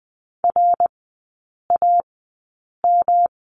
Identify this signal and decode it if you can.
Morse code, heard as a keyed tone, about 20 wpm, spaced as for 8 wpm, 700 Hz. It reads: RAM
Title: 107 Lesson 18e - Letter m - Three Characters 20wpm